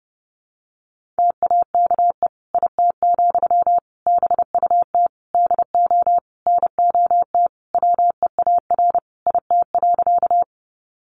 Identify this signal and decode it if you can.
Morse code, heard as a keyed tone, about 30 wpm, 700 Hz. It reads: TAKE IT, BUT DO NOT WEAR IT.
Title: Sentences from Top 1500 Words 30wpm